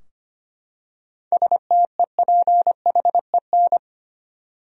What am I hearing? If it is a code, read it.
Morse code, heard as a keyed tone, about 25 wpm, 700 Hz. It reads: STEPHEN
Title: USA Common Names - 25wpm